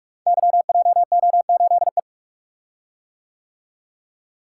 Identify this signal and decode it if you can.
Morse code, heard as a keyed tone, about 44 wpm, 700 Hz. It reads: YJO8E
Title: Code Groups - 5 Letter:Numbers - 45wpm